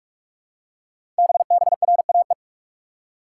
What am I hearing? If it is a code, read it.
Morse code, heard as a keyed tone, about 44 wpm, 700 Hz. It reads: BBRAE
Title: Code Groups - 5 Letters - 45wpm